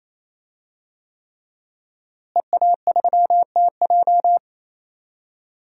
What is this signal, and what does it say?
Morse code, heard as a keyed tone, about 27 wpm, 700 Hz. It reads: EA3TJ
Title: DXCC-100 Call Signs - 5 Characters - Repeated 3x - 28wpm